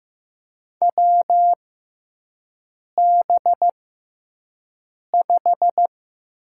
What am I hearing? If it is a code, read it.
Morse code, heard as a keyed tone, about 15 wpm, spaced as for 5 wpm, 700 Hz. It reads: WB5
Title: USA Call Signs - 6 Characters - 6x Character Spacing - 15wpm